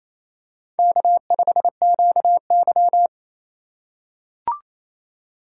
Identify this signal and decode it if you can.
Morse code, heard as a keyed tone, about 28 wpm, 700 Hz. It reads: K5QY
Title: USA Call Signs - 4 Characters - 2x Character Spacing - 28wpm